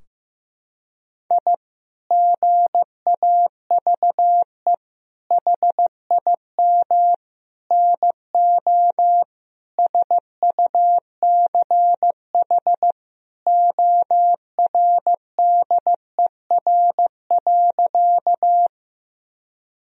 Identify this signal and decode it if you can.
Morse code, heard as a keyed tone, about 15 wpm, 700 Hz. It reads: I GAVE HIM NO SUCH ORDER.
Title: Sentences from Top 1000 Words - Review Part 2 - 15wpm